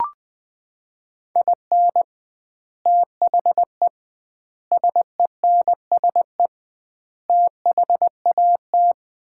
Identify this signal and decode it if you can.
Morse code, heard as a keyed tone, about 20 wpm, 700 Hz. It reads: IN THE SENSE THAT
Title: 2x Word-Spacing - Sets of 4 Words - 20wpm